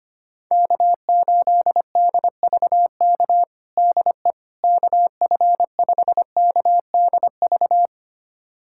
Morse code, heard as a keyed tone, about 25 wpm, 700 Hz. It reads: K8DVK DE KF5KDV
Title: USA Callsign DE Callsign 25wpm